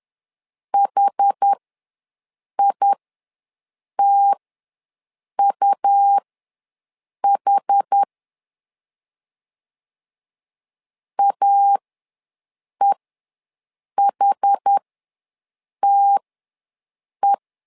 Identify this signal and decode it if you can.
Morse code, heard as a keyed tone, about 10 wpm, spaced as for 6 wpm, 800 Hz. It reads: HITUH AEHTE
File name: LEZIONI / 2 UHDN / 2 UH ws 10wpm 2029